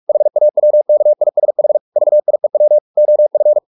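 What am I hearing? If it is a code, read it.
Morse code, heard as a keyed tone, about 44 wpm, 600 Hz. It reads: HAWKISH VIEW OF
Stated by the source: Morse practice at 45 wpm, the novel Never by Ken Follett in English, from CW med Gnister.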